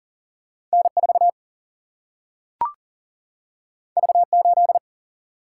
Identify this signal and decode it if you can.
Morse code, heard as a keyed tone, about 40 wpm, 700 Hz. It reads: N 4 V 8
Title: Single-Letter Single-Number 40wpm